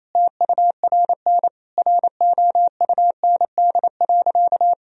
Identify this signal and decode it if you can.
Morse code, heard as a keyed tone, about 28 wpm, 700 Hz. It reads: TURN ROUND.
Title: Sentences from Top 1000 Words - Review Part 1 - 28wpm